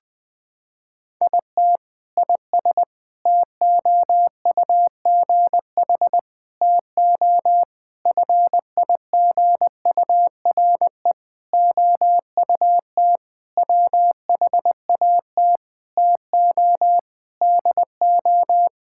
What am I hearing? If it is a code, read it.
Morse code, heard as a keyed tone, about 20 wpm, 700 Hz. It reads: IT IS TOUGH TO FIGURE OUT WHAT TO DO
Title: Sentences with 6-Word Phrases - 20wpm